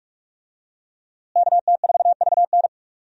Morse code, heard as a keyed tone, about 45 wpm, 700 Hz. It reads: KT4VN
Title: USA Call Signs - 5 Characters - Repeated 2x - 45wpm